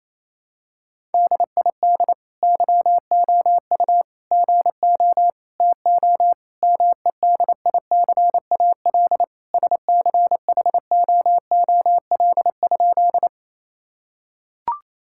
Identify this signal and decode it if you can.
Morse code, heard as a keyed tone, about 28 wpm, 700 Hz. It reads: DID YOU GO TO MEDICAL SCHOOL?
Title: Sentences from Top 1000 Words - Review Part 5 - 28wpm